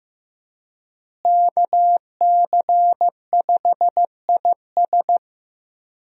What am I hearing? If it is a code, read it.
Morse code, heard as a keyed tone, about 16 wpm, 700 Hz. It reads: KC5IS
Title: USA Call Signs - 5 Characters - Repeated 3x - 15wpm